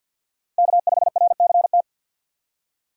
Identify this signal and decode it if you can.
Morse code, heard as a keyed tone, about 65 wpm, spaced as for 57 wpm, 700 Hz. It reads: K5RCT
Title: Speed Racing - USA Call Signs 50wpm